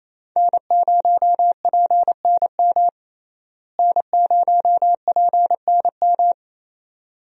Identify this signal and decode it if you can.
Morse code, heard as a keyed tone, about 28 wpm, 700 Hz. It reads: N0PNM N0PNM
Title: USA Call Signs - 5 Characters - Repeated 2x - 28wpm